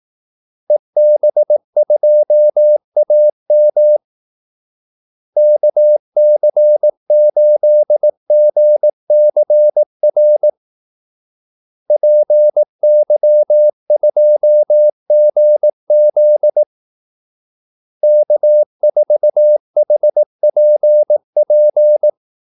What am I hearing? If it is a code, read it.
Morse code, heard as a keyed tone, about 18 wpm, 600 Hz. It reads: EB2AM KC8GCR PY2GZ K4HPP
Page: Kaldesignaler 18 wpm